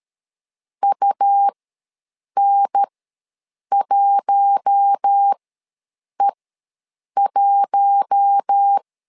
Morse code, heard as a keyed tone, about 13 wpm, spaced as for 7 wpm, 800 Hz. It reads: UN1E1
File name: LEZIONI / 4 Q1S ws 12wpm 1827